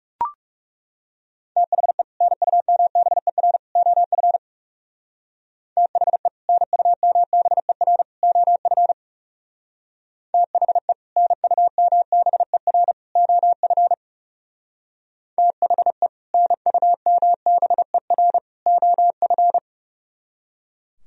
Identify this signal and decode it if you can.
Morse code, heard as a keyed tone, about 37 wpm, 700 Hz. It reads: THE NUMBER OF THE NUMBER OF THE NUMBER OF THE NUMBER OF
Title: Speed-Racing - Sets of 3 Words 30wpm